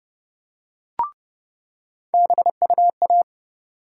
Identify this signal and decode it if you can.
Morse code, heard as a keyed tone, about 30 wpm, 700 Hz. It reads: BUA
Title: USA Call Sign Postfixes - 3 Letters 30wpm